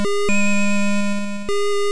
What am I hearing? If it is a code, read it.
There is no code.